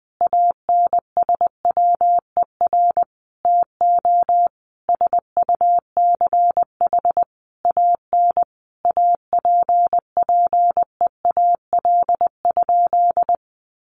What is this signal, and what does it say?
Morse code, heard as a keyed tone, about 20 wpm, 700 Hz. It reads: ANSWER TO SUCH AN APPEAL?
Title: Sentences from Top 1300 Words 20wpm